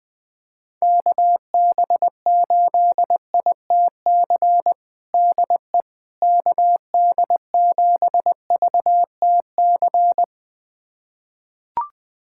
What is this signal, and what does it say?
Morse code, heard as a keyed tone, about 21 wpm, 700 Hz. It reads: KB8ITC DE KD7VTC
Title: USA Callsign DE Callsign - 6 Characters - 20wpm